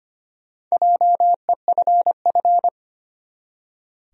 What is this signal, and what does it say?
Morse code, heard as a keyed tone, about 25 wpm, 700 Hz. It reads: JEFF
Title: USA Common Names - Max 5 Letters - 25wpm